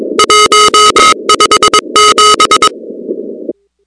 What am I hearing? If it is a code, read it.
Morse code, heard as a keyed tone, about 22 wpm, 1270 Hz, with noise. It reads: 157
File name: contests / 2019-cw-wpx-kt5j-k5pi- -2330 0-37